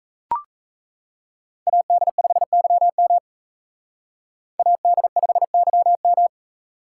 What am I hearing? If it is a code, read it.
Morse code, heard as a keyed tone, about 35 wpm, 700 Hz. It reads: AD5YM AD5YM